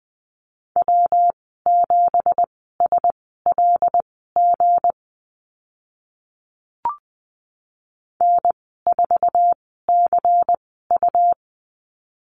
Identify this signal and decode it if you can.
Morse code, heard as a keyed tone, about 20 wpm, spaced as for 15 wpm, 700 Hz. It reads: W7SLG N4CU
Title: 186 Lesson 30g - Letter b - Call Signs 20wpm